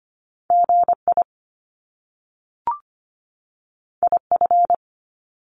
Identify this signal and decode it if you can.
Morse code, heard as a keyed tone, about 25 wpm, 700 Hz. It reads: GI IF
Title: Top Two Letter Combinations in Words 25wpm